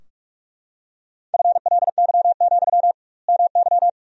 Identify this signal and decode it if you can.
Morse code, heard as a keyed tone, about 41 wpm, 700 Hz. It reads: FLY, MY
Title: 2x Word-Spacing - Sentences from Top 500 Words - 45wpm